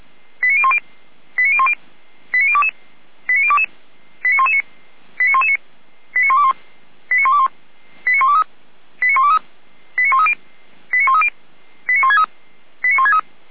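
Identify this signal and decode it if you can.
Five-tone selective call (ZVEI), heard as quick runs of five tones, tones of about 70 ms. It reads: 89010 89010 89022 89022 89119 89119 89121 89121 89123 89123 89130 89130 89163 89163
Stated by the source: ZVEI five-tone selective call 89011 received on SDR.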